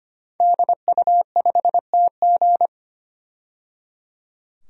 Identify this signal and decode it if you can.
Morse code, heard as a keyed tone, about 25 wpm, 700 Hz. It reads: DU5TG
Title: DXCC-100 Call Signs - 5 Characters - 25wpm